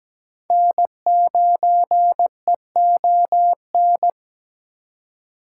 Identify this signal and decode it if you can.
Morse code, heard as a keyed tone, about 20 wpm, 700 Hz. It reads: N9EON
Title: Speed Racing - USA 1x3 Call Signs 17wpm